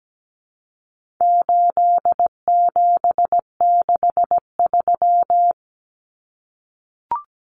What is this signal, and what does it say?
Morse code, heard as a keyed tone, about 17 wpm, 700 Hz. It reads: 8763
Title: Common Cut Numbers - 4-Digit 17wpm